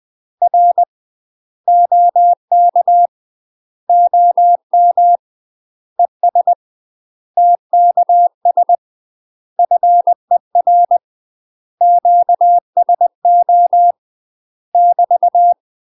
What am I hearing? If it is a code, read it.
Morse code, heard as a keyed tone, about 20 wpm, 700 Hz. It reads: R OK OM ES TKS FER QSO <BT>
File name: UK MASTER QSOs TRAINING / 20WPM / UK MASTER QSOs 0070